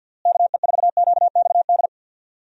Morse code, heard as a keyed tone, about 51 wpm, 700 Hz. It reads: KE4QXD
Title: USA Call Signs - 6 Characters - Repeated 3x - 50wpm